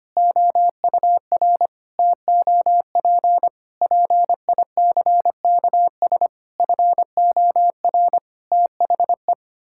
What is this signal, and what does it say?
Morse code, heard as a keyed tone, about 25 wpm, 700 Hz. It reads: OUR TOP PICKS FOR THE
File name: Sets of 9 Words - 25wpm